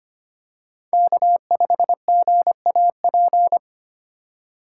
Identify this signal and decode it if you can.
Morse code, heard as a keyed tone, about 25 wpm, 700 Hz. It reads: K5GAP
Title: USA Call Signs - 5 Characters - 2x Character Spacing - 25wpm